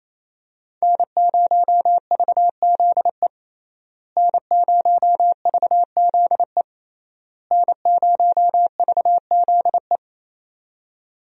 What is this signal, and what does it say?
Morse code, heard as a keyed tone, about 28 wpm, 700 Hz. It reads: N0VZE N0VZE N0VZE
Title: USA Call Signs - 5 Characters - Repeated 3x - 28wpm